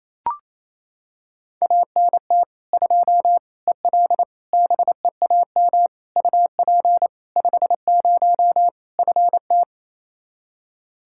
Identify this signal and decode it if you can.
Morse code, heard as a keyed tone, about 27 wpm, 700 Hz. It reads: ANT 2 EL BEAM UP 50 FT
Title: Practice QSO - Exchange 2 - Info - 28wpm